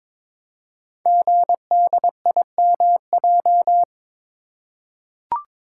Morse code, heard as a keyed tone, about 23 wpm, 700 Hz. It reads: GDIMJ
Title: Code Groups - 5 Letters - 22wpm